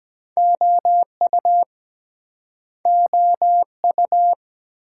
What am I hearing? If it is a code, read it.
Morse code, heard as a keyed tone, about 22 wpm, 700 Hz. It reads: OU OU